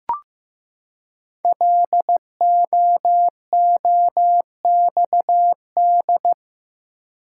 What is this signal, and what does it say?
Morse code, heard as a keyed tone, about 15 wpm, 700 Hz. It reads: LOOXD
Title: Code Groups - 5 Letter:Numbers - 15wpm